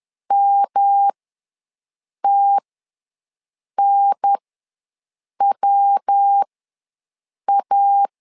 Morse code, heard as a keyed tone, about 10 wpm, spaced as for 6 wpm, 800 Hz. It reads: MTNWA
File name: LEZIONI / 3 MWRO / 3 MW ws 10wpm 2213